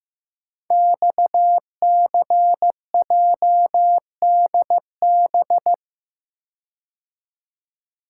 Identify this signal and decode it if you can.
Morse code, heard as a keyed tone, about 15 wpm, 700 Hz. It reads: XCJDB
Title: Code Groups - 5 Letters - 15wpm